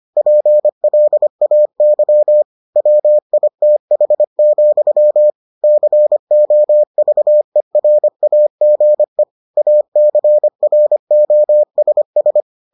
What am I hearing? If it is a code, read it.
Morse code, heard as a keyed tone, about 25 wpm, 600 Hz. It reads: PLAY WITH, COVERAGE ACROSS